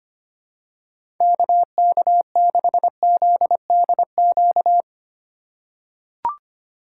Morse code, heard as a keyed tone, about 25 wpm, 700 Hz. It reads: KK6ZDQ
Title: USA Call Signs - 6 Characters - 4x Character Spacing - 25wpm